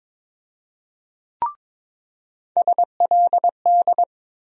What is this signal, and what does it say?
Morse code, heard as a keyed tone, about 22 wpm, 700 Hz. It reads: SLD
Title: USA Call Sign Postfixes - 3 Letters 22wpm